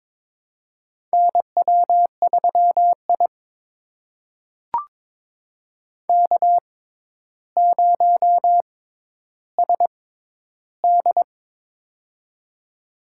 Morse code, heard as a keyed tone, about 22 wpm, spaced as for 11 wpm, 700 Hz. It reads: NW3I K0SD
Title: USA Call Signs - 4 Characters - 6x Character Spacing - 22wpm